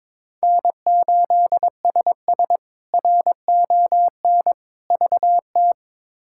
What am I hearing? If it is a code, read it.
Morse code, heard as a keyed tone, about 21 wpm, 700 Hz. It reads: N8SS RON VT
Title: K1USN SST QSO Practice - 22wpm